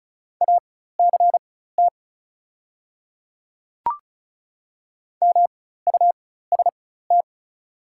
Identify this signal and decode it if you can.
Morse code, heard as a keyed tone, about 36 wpm, spaced as for 16 wpm, 700 Hz. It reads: ACT MUST
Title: 4x Character Spacing - Top 200 Words - 35wpm